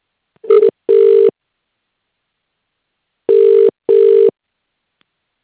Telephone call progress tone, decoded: ringback tone